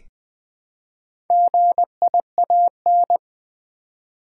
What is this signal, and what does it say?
Morse code, heard as a keyed tone, about 20 wpm, 700 Hz. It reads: GIAN
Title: Build A Word - 1 Letter At A Time - 3 to 5-Letter Words - 20wpm